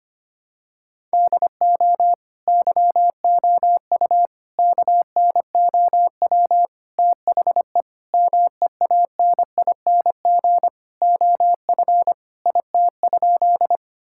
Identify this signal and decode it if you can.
Morse code, heard as a keyed tone, about 25 wpm, 700 Hz. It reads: DO YOU KNOW THE MEANING OF IT?
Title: Sentences from Top 1000 Words - Review Part 6 - 25wpm